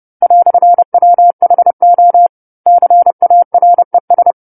Morse code, heard as a keyed tone, about 30 wpm, 700 Hz. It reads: "WHO CARES